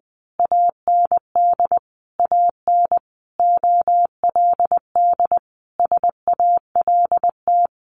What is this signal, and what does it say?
Morse code, heard as a keyed tone, about 20 wpm, 700 Hz. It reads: AND AN OLD SALT
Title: A Sound Of Thunder 20wpm